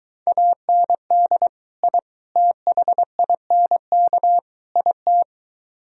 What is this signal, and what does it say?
Morse code, heard as a keyed tone, about 23 wpm, 700 Hz. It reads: AND I THINK IT